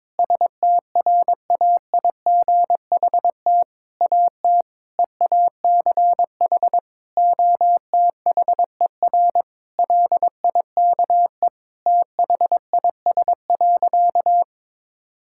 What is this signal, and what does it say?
Morse code, heard as a keyed tone, about 21 wpm, 700 Hz. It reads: STRAIGHT AT EACH OTHER LIKE THIS.
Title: Sentences from Top 1000 Words - Review Part 6 - 22wpm